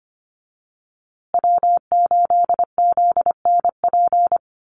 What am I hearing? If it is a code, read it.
Morse code, heard as a keyed tone, about 25 wpm, 700 Hz. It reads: W8ZNP
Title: USA Call Signs Repeated 3x 25wpm